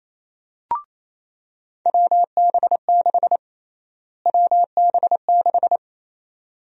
Morse code, heard as a keyed tone, about 28 wpm, 700 Hz. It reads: WB6 WB6
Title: USA Call Sign Prefixes - Repeated 2x - 28wpm